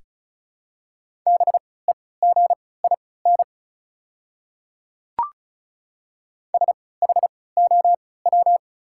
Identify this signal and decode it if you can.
Morse code, heard as a keyed tone, about 34 wpm, spaced as for 20 wpm, 690 Hz. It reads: BEGIN SHOW
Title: 3x Character Spacing - Top 300 Words - 35wpm